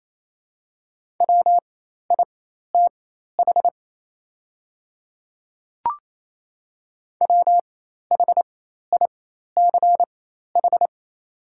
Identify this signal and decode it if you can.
Morse code, heard as a keyed tone, about 28 wpm, spaced as for 13 wpm, 700 Hz. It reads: WITH WHICH
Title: 4x Character Spacing - Top 100 Words - 28wpm